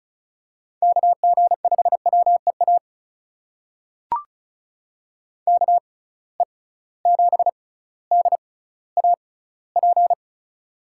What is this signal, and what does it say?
Morse code, heard as a keyed tone, about 36 wpm, spaced as for 18 wpm, 700 Hz. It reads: KG5WEA KE7DAP
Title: USA Call Signs - 6 Characters - 6x Character Spacing - 35wpm